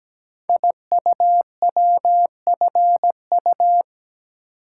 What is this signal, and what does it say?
Morse code, heard as a keyed tone, about 17 wpm, 700 Hz. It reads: IUWFU
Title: Code Groups - 5 Letters - 17wpm